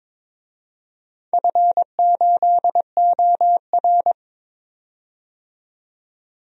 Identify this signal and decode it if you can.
Morse code, heard as a keyed tone, about 23 wpm, 700 Hz. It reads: F8OR
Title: DXCC-100 Call Signs - 4 Characters - 22wpm